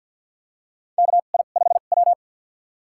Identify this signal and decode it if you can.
Morse code, heard as a keyed tone, about 53 wpm, spaced as for 37 wpm, 700 Hz. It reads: KI5W
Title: USA Call Signs - 4 Characters - 2x Character Spacing - 50wpm